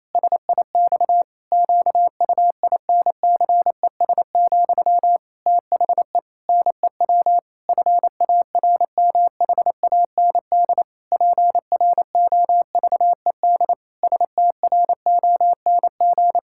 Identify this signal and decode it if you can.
Morse code, heard as a keyed tone, about 28 wpm, 700 Hz. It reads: SIX QUINCES, THE NEW FARMHAND PROVED STRONG
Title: Pangram Challenge v2 - 28wpm